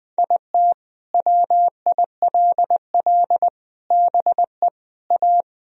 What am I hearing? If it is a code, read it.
Morse code, heard as a keyed tone, about 20 wpm, 700 Hz. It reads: IT WILL BE A
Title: Sentences from Top 1000 Words - Review Part 3 - 20wpm